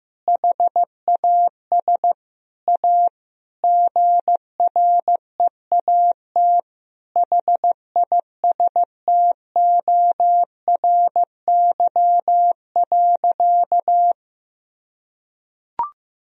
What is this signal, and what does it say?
Morse code, heard as a keyed tone, about 16 wpm, 700 Hz. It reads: HAS A GREAT HISTORY.
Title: Sentences from Top 1000 Words - Review Part 4 - 15wpm